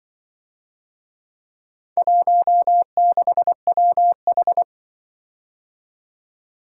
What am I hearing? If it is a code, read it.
Morse code, heard as a keyed tone, about 25 wpm, 700 Hz. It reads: 16WH